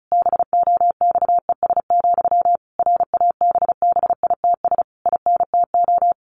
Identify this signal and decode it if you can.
Morse code, heard as a keyed tone, about 34 wpm, 700 Hz. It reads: BOXES, RABBITS INTO
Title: A Sound Of Thunder 35wpm